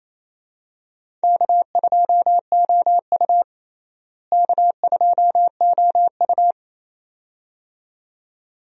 Morse code, heard as a keyed tone, about 28 wpm, 700 Hz. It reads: K2OU K2OU
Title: USA Call Signs - 4 Characters - Repeated 2x - 28wpm